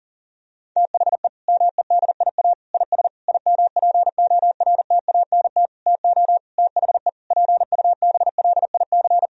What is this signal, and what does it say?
Morse code, heard as a keyed tone, about 40 wpm, 700 Hz. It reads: THE MEDIA IS IMPORTANT TO THE PUBLIC
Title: Sentences from Top 1000 Words - Review Part 4 - 40wpm